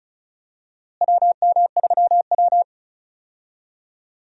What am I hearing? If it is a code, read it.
Morse code, heard as a keyed tone, about 34 wpm, 700 Hz. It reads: WM3W
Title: USA Call Signs - 4 Characters - Repeated 2x - 35wpm